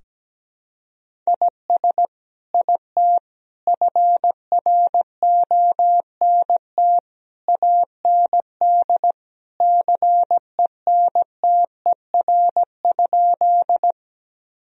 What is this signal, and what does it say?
Morse code, heard as a keyed tone, about 17 wpm, 700 Hz. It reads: IS IT FRONT AND CENTER?
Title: Sentences from Top 1000 Words - Review Part 2 - 17wpm